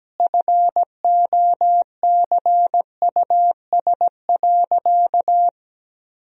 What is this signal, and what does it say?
Morse code, heard as a keyed tone, about 17 wpm, 700 Hz. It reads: FOCUS.
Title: Sentences from Top 1000 Words - Review Part 4 - 17wpm